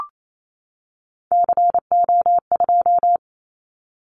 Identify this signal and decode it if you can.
Morse code, heard as a keyed tone, about 28 wpm, 700 Hz. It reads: CO2
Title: Callsign Prefixes for Top 100 DXCC - 28wpm